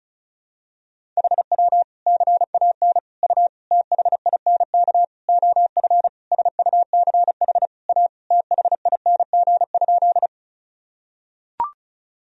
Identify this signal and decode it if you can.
Morse code, heard as a keyed tone, about 36 wpm, 700 Hz. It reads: HW CAN U THINK OF SUCH A THING?
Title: Sentences from Top 200 Words and Abbreviations 35wpm